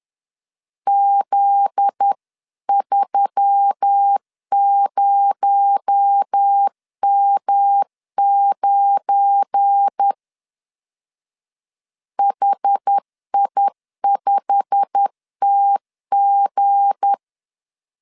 Morse code, heard as a keyed tone, about 10 wpm, 800 Hz. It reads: Z30M9 HI5TG
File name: CW / DET / 1 9 1500C S B 10wpm 4439